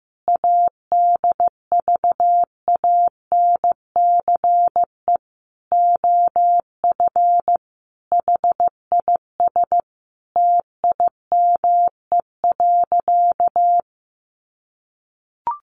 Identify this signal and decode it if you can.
Morse code, heard as a keyed tone, about 15 wpm, 700 Hz. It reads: ADVANCE OF HIS TIME.
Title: Sentences from Top 1400 Words 15wpm